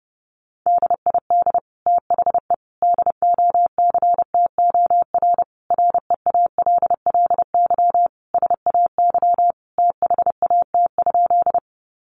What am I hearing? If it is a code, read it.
Morse code, heard as a keyed tone, about 30 wpm, 700 Hz. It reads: DID THE DOCTOR REALLY SAY THAT?
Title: Sentences from Top 1500 Words 30wpm